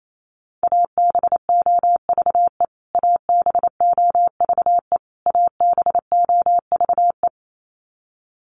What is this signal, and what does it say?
Morse code, heard as a keyed tone, about 28 wpm, 700 Hz. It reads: ABOVE ABOVE ABOVE
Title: Top 500 Words Repeated 28wpm